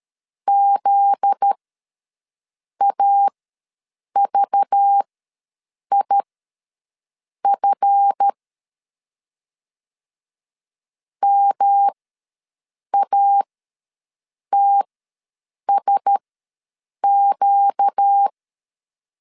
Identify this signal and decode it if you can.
Morse code, heard as a keyed tone, about 13 wpm, spaced as for 6 wpm, 800 Hz. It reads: ZAVIF MATSQ
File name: 7 GF9Z ws 12wpm